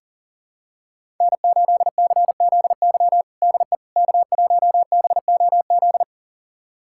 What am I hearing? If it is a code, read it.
Morse code, heard as a keyed tone, about 40 wpm, 700 Hz. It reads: N8CZY DE K1BOZ
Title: USA Callsign DE Callsign - 5 Characters - 40wpm